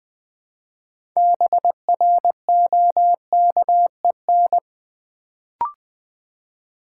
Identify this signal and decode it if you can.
Morse code, heard as a keyed tone, about 20 wpm, 700 Hz. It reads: BROKEN
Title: Build A Word - 2 Letters At A Time - 6-Letter Words - 20wpm